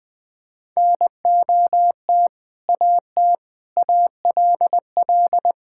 Morse code, heard as a keyed tone, about 20 wpm, 700 Hz. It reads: NOT AT ALL
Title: Sentences with 2-Word Phrases - 20wpm